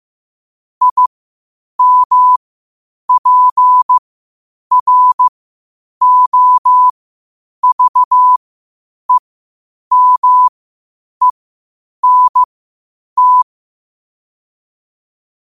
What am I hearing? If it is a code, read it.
Morse code, heard as a keyed tone, about 15 wpm, spaced as for 8 wpm, 1000 Hz. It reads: IMPROVEMENT
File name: qotdfiles / 8wpm